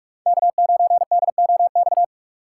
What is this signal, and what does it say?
Morse code, heard as a keyed tone, about 46 wpm, 700 Hz. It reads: K9DOX
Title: USA Call Signs - 2x Character Spacing - 45wpm